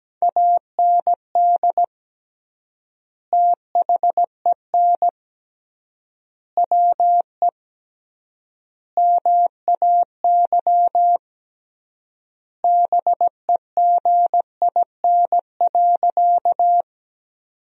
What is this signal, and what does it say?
Morse code, heard as a keyed tone, about 17 wpm, 700 Hz. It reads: AND THEN WE MAY BEGIN.
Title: 3x Word-Spacing - Sentences from Top 300 Words - 17wpm